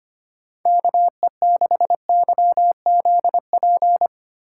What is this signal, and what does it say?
Morse code, heard as a keyed tone, about 31 wpm, 700 Hz. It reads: KE6YZP
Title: Speed Racing - USA 2x3 Call Signs 25wpm